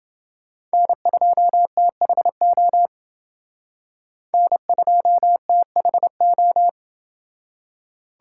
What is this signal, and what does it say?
Morse code, heard as a keyed tone, about 29 wpm, 700 Hz. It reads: N2THO N2THO